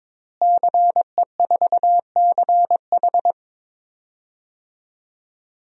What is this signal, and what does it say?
Morse code, heard as a keyed tone, about 21 wpm, 700 Hz. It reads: CE4CH
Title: DXCC-100 Call Signs - 5 Characters - Repeated 2x - 22wpm